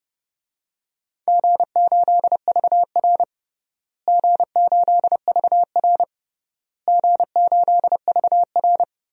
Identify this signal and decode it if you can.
Morse code, heard as a keyed tone, about 32 wpm, 700 Hz. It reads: G8VR G8VR G8VR
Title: DXCC-100 Call Signs - 4 Characters - Repeated 3x - 30wpm